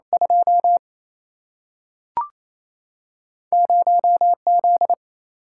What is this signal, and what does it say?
Morse code, heard as a keyed tone, about 28 wpm, 700 Hz. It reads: 2 0 Z
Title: Single-Letter Single-Number 28wpm